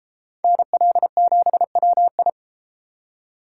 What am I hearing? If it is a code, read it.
Morse code, heard as a keyed tone, about 32 wpm, 700 Hz. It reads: NL7WI